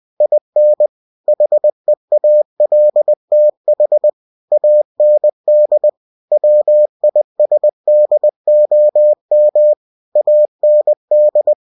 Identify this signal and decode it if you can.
Morse code, heard as a keyed tone, about 20 wpm, 600 Hz. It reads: IN HEALTH AND WISDOM AND